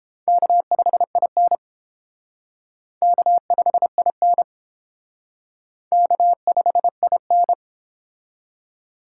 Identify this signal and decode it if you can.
Morse code, heard as a keyed tone, about 30 wpm, 700 Hz. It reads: K5IN K5IN K5IN